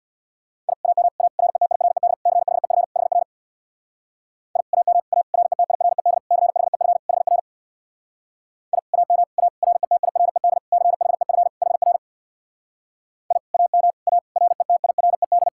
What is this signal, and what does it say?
Morse code, heard as a keyed tone, about 67 wpm, 700 Hz. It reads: I AM A RETIRED CHEF HR I AM A RETIRED CHEF HR I AM A RETIRED CHEF HR I AM A RETIRED